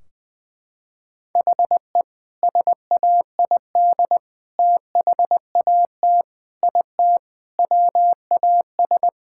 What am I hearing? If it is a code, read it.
Morse code, heard as a keyed tone, about 21 wpm, 700 Hz. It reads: HE SAID THAT IT WAS
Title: Sentences from Top 1600 Words 20wpm